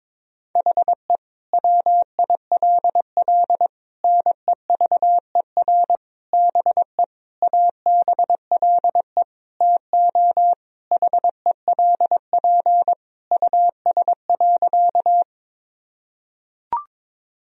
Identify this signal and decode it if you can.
Morse code, heard as a keyed tone, about 22 wpm, 700 Hz. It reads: HE WILL NEVER BE ABLE TO HELP US.
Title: Sentences from Top 1000 Words - Review Part 2 - 22wpm